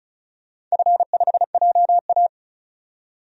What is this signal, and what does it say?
Morse code, heard as a keyed tone, about 36 wpm, 700 Hz. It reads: F5JA
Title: DXCC-100 Call Signs - 4 Characters - 35wpm